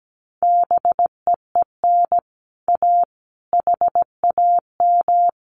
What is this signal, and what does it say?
Morse code, heard as a keyed tone, about 17 wpm, 700 Hz. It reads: BEEN A HAM
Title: Common QSO Phrases Encore 17wpm